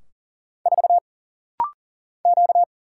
Morse code, heard as a keyed tone, about 40 wpm, 700 Hz. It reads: 4 Q
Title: Single Letter-Number - Rapid-Fire 40wpm